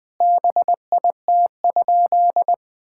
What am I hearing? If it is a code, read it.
Morse code, heard as a keyed tone, about 20 wpm, 700 Hz. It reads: BIT?
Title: Sentences with 5-Word Phrases - 20wpm